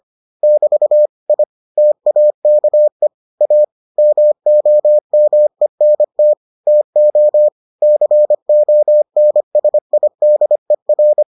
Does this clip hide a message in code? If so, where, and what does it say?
Yes: Morse code from 0.4 to 11.2 s, heard as a keyed tone, about 25 wpm, 600 Hz. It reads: <BT> I TAKE A MOMENT TO CONSIDER